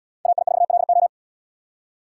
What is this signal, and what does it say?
Morse code, heard as a keyed tone, about 73 wpm, 700 Hz. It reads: AI4LO